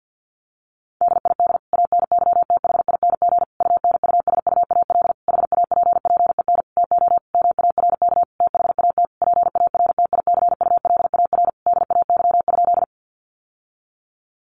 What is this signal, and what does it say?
Morse code, heard as a keyed tone, about 50 wpm, 700 Hz. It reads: DID ANYTHING UNUSUAL HAPPEN TO MARK THAT PARTICULAR DAY?
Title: Sentences from Top 1400 Words 50wpm